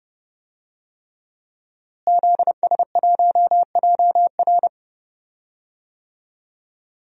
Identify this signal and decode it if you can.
Morse code, heard as a keyed tone, about 30 wpm, 700 Hz. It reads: ZS1JR
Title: DXCC-100 Call Signs - 5 Characters - Repeated 3x - 30wpm